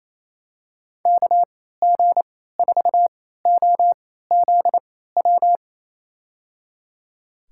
Morse code, heard as a keyed tone, about 28 wpm, spaced as for 16 wpm, 700 Hz. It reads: KG4OZW
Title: USA Call Signs - 6 Characters - 3x Character Spacing - 28wpm